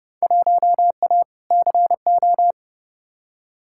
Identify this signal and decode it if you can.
Morse code, heard as a keyed tone, about 30 wpm, 700 Hz. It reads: 1A CO
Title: Field Day Exchanges - 30wpm